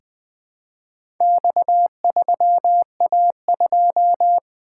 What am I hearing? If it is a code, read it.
Morse code, heard as a keyed tone, about 20 wpm, 700 Hz. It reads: X3A2
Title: Four Letter-Number Combinations - 20wpm